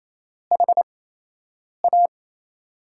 Morse code, heard as a keyed tone, about 28 wpm, spaced as for 8 wpm, 700 Hz. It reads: HA
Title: USA Call Signs - 4 Characters - 8x Character Spacing - 28wpm